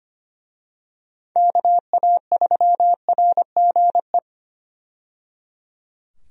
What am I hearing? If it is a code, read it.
Morse code, heard as a keyed tone, about 25 wpm, 700 Hz. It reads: KA3RGE
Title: USA Call Signs - 6 Characters - 25wpm